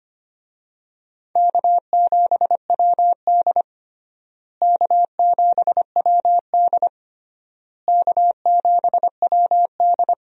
Morse code, heard as a keyed tone, about 25 wpm, 700 Hz. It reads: K7WD K7WD K7WD
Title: USA Call Signs - 4 Characters - Repeated 3x - 25wpm